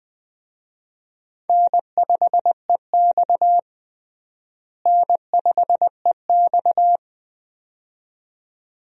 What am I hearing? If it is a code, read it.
Morse code, heard as a keyed tone, about 20 wpm, 700 Hz. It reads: N5EX N5EX
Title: USA Call Signs - 4 Characters - Repeated 2x - 20wpm